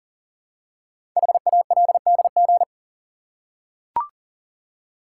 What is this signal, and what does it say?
Morse code, heard as a keyed tone, about 41 wpm, 700 Hz. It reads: HALDG
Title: Code Groups - 5 Letters - 40wpm